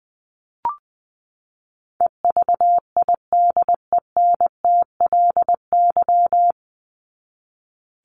Morse code, heard as a keyed tone, about 20 wpm, 700 Hz. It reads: EVIDENTLY
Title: LY Suffix - Top 100 Words with Base Words 20wpm